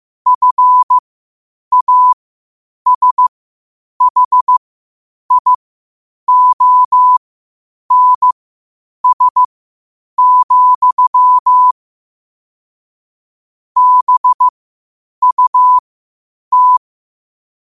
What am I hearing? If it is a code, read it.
Morse code, heard as a keyed tone, about 15 wpm, spaced as for 8 wpm, 1000 Hz. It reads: FASHIONS, BUT
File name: qotdfiles / 8wpm